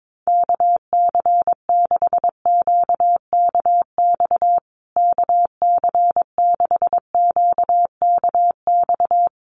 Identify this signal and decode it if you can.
Morse code, heard as a keyed tone, about 22 wpm, 700 Hz. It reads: KC6QKX KC6QKX
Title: USA Call Signs Repeated 3x 22wpm